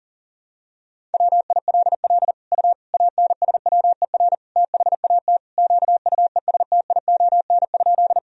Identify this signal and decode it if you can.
Morse code, heard as a keyed tone, about 37 wpm, 700 Hz. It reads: WILL U ANSWER THAT QUESTION?
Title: Sentences from Top 400 Words and Abbreviations 40wpm